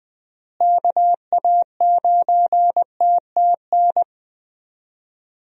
Morse code, heard as a keyed tone, about 20 wpm, 700 Hz. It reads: KA9TTN
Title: USA Call Signs - 6 Characters - Repeated 2x - 20wpm